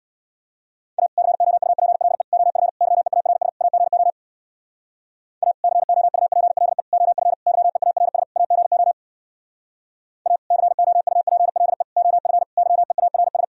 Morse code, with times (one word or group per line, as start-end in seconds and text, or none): 1.0-1.1 A
1.2-2.2 COUPLE
2.3-2.7 OF
2.8-3.5 YEARS
3.6-4.1 AGO
5.4-5.5 A
5.6-6.8 COUPLE
6.9-7.3 OF
7.5-8.2 YEARS
8.4-8.9 AGO
10.3-10.4 A
10.5-11.8 COUPLE
12.0-12.4 OF
12.6-13.4 YEARS